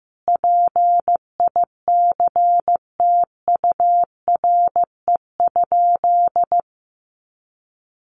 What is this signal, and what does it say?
Morse code, heard as a keyed tone, about 15 wpm, 700 Hz. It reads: PICTURE?
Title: A Sound Of Thunder 15wpm